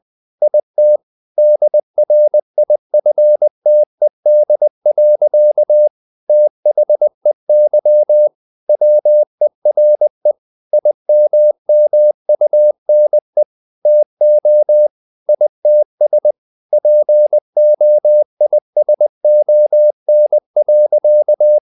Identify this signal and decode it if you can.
Morse code, heard as a keyed tone, about 20 wpm, 600 Hz. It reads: IT DRIFTED. THEY WERE IMMUNE TO ITS POISON.
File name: the old man and the sea / the old man and the sea part 1 20wpm